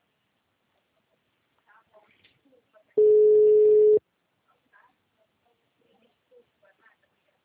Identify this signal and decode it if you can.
Ringback tone (Europe, 425 Hz).
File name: out- -2020- - - .79305